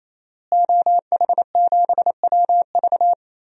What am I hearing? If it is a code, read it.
Morse code, heard as a keyed tone, about 28 wpm, 700 Hz. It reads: OH7WV
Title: DXCC-100 Call Signs - 5 Characters - Repeated 3x - 28wpm